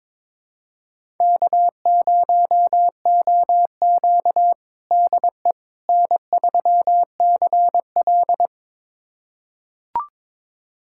Morse code, heard as a keyed tone, about 22 wpm, 700 Hz. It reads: K0OQ DE N3CL
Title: USA Callsign DE Callsign - 4 Characters - 22wpm